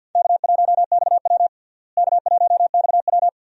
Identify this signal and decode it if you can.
Morse code, heard as a keyed tone, about 53 wpm, 700 Hz. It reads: K1XW K1XW
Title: USA Call Signs - 4 Characters - Repeated 2x - 50wpm